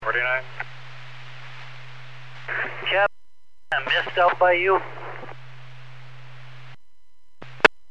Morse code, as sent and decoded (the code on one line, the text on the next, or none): none